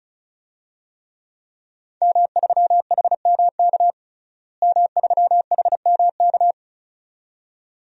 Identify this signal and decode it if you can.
Morse code, heard as a keyed tone, about 36 wpm, 700 Hz. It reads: M3HMK M3HMK
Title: DXCC-100 Call Signs - 5 Characters - Repeated 2x - 35wpm